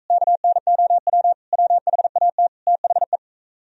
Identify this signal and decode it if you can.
Morse code, heard as a keyed tone, about 41 wpm, 700 Hz. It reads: KNOW WHAT THE